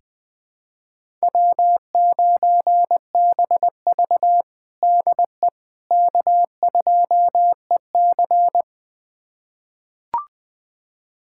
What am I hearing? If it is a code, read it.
Morse code, heard as a keyed tone, about 20 wpm, 700 Hz. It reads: W9BV DE K2EC
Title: USA Callsign DE Callsign - 4 Characters - 20wpm